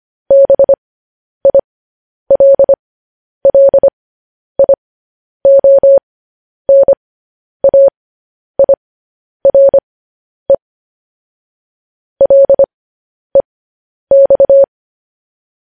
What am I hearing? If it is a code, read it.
Morse code, heard as a keyed tone, about 25 wpm, spaced as for 10 wpm, 550 Hz. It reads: BILLIONAIRE LEX
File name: podcast / News headlines-1x-25x10